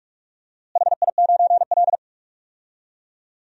Morse code, heard as a keyed tone, about 46 wpm, 700 Hz. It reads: HI9L
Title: DXCC-100 Call Signs - 4 Characters - Repeated 2x - 45wpm